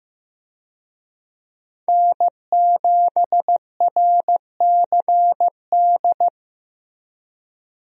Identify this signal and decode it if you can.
Morse code, heard as a keyed tone, about 15 wpm, 700 Hz. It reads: N7RCD
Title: USA Call Signs - 5 Characters - Repeated 3x - 15wpm